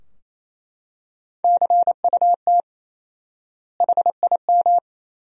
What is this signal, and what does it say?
Morse code, heard as a keyed tone, about 28 wpm, 700 Hz. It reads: CUT HIM
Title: 4x Word-Spacing - Sentences from Top 300 Words - 28wpm